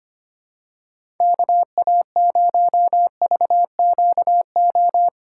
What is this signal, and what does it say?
Morse code, heard as a keyed tone, about 25 wpm, 700 Hz. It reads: KA0VQO
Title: USA Call Signs - 6 Characters - 4x Character Spacing - 25wpm